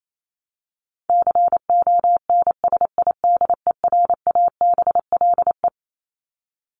Morse code, heard as a keyed tone, about 28 wpm, 700 Hz. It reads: CONSIDERABLE
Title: LY Suffix - Top 100 Words with Base Words 28wpm